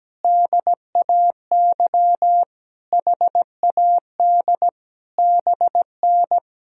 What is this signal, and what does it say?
Morse code, heard as a keyed tone, about 17 wpm, 700 Hz. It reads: DAY HAD BN
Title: Sentences from Top 500 Words and Abbreviations 17wpm